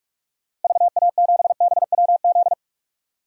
Morse code, heard as a keyed tone, about 46 wpm, 700 Hz. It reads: VA7BWZ
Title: DXCC-100 Call Signs - 6 Characters - Repeated 2x - 45wpm